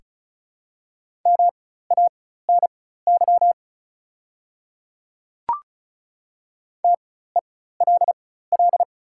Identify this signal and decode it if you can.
Morse code, heard as a keyed tone, about 35 wpm, spaced as for 16 wpm, 700 Hz. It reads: MANY TELL
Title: 4x Character Spacing - Top 200 Words - 35wpm